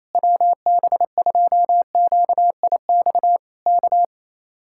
Morse code, heard as a keyed tone, about 27 wpm, 700 Hz. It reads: WB2QIX K
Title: Speed-Racing Common QSO Phrases - 28wpm